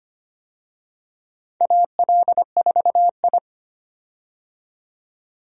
Morse code, heard as a keyed tone, about 25 wpm, 700 Hz. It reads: AL4I
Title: USA Call Signs - 4 Characters - 25wpm